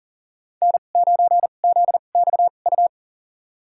Morse code, heard as a keyed tone, about 40 wpm, spaced as for 29 wpm, 700 Hz. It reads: N9ZXU
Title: USA Call Signs - 2x Character Spacing - 40wpm